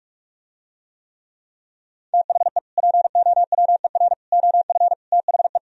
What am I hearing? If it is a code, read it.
Morse code, heard as a keyed tone, about 47 wpm, 700 Hz. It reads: THE POWER OF THE